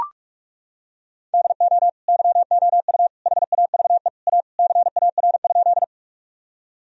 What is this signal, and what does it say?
Morse code, heard as a keyed tone, about 42 wpm, 700 Hz. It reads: DO YOU HAVE A CAR?
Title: Sentences from Top 1000 Words - Review Part 1 - 45wpm